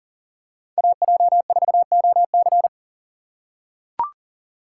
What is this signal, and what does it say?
Morse code, heard as a keyed tone, about 40 wpm, 700 Hz. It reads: AJ4OC
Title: USA Call Signs - 5 Characters - 4x Character Spacing - 40wpm